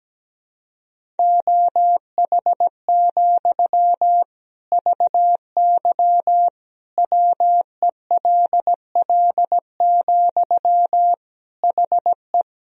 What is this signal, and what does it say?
Morse code, heard as a keyed tone, about 17 wpm, 700 Hz. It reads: OH, VY WELL, HE
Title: Sentences from Top 500 Words and Abbreviations 17wpm